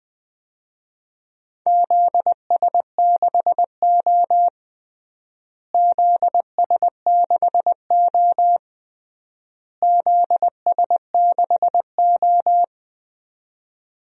Morse code, heard as a keyed tone, about 21 wpm, 700 Hz. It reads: ZS6O ZS6O ZS6O
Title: DXCC-100 Call Signs - 4 Characters - Repeated 3x - 20wpm